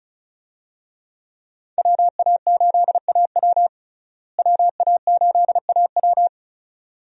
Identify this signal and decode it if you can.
Morse code, heard as a keyed tone, about 34 wpm, 700 Hz. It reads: WA8AW WA8AW
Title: USA Call Signs - 5 Characters - Repeated 2x - 35wpm